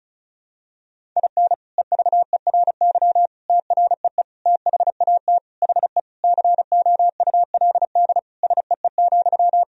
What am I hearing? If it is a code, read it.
Morse code, heard as a keyed tone, about 36 wpm, 700 Hz. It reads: IN EVERY TREE THAT HE COULD SEE,
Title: Sentences from Top 1000 Words - Review Part 1 - 35wpm